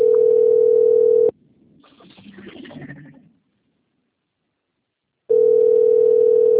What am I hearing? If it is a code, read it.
Ringback tone (North America), 440 + 480 Hz.